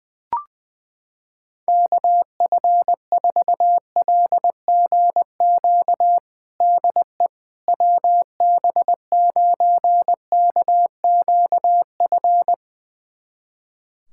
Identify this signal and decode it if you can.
Morse code, heard as a keyed tone, about 20 wpm, 700 Hz. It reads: KF4LGQ DE WB9KQF
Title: USA Callsign DE Callsign - 6 Characters - 20wpm